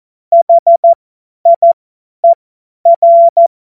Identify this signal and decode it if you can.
Morse code, heard as a keyed tone, about 14 wpm, spaced as for 10 wpm, 680 Hz. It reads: HIER
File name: Morsekurs / Witze / Witz 14. März 2023 50 F70